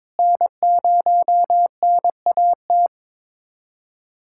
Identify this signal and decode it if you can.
Morse code, heard as a keyed tone, about 21 wpm, 700 Hz. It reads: N0NAT
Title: USA Callsign DE Callsign - 5 Characters - 22wpm